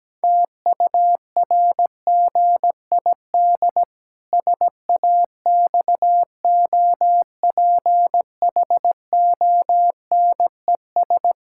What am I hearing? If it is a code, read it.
Morse code, heard as a keyed tone, about 17 wpm, 700 Hz. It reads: TURGID SAXOPHONES
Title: Pangram Challenge v2 - 17wpm